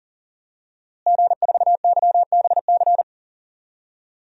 Morse code, heard as a keyed tone, about 40 wpm, 700 Hz. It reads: G4YBC
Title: DXCC-100 Call Signs - 5 Characters - Repeated 2x - 40wpm